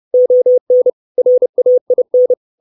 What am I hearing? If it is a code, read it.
Morse code, heard as a keyed tone, about 30 wpm, 500 Hz. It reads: ON RAIN
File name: 632 words / 632 Words 30WPM